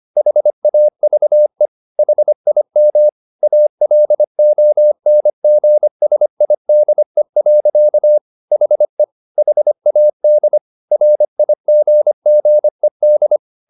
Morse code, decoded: HAVE HIM ALONGSIDE. HE HAD RIGGED

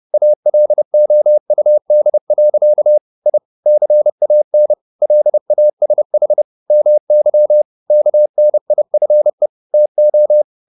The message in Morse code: ALOUD. I CAN LASH MY KNIFE TO

